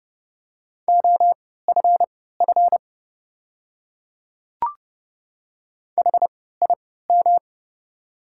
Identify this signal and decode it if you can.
Morse code, heard as a keyed tone, about 30 wpm, spaced as for 17 wpm, 690 Hz. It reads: OFF HIM
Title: 3x Character Spacing - Top 200 Words - 30wpm